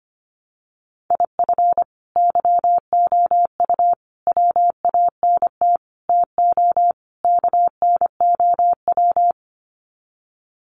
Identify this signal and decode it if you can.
Morse code, heard as a keyed tone, about 25 wpm, 700 Hz. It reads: IF YOU WANT TO KNOW
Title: Sets of 5 Words Encore2 25wpm